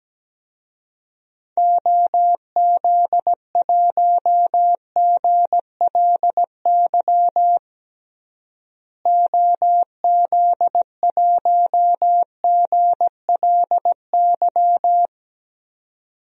Morse code, heard as a keyed tone, about 17 wpm, 700 Hz. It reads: OZ1GLY OZ1GLY
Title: DXCC-100 Call Signs - 6 Characters - Repeated 2x - 17wpm